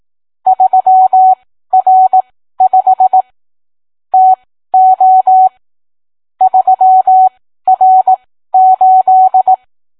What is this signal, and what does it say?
Morse code, heard as a keyed tone, about 18 wpm, spaced as for 13 wpm, 750 Hz. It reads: 3R5 TO 3R8